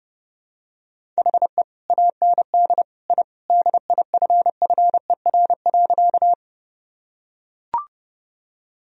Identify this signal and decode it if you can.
Morse code, heard as a keyed tone, about 30 wpm, 700 Hz. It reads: HE AND I DIFFER.
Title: Sentences from Top 1000 Words - Review Part 1 - 30wpm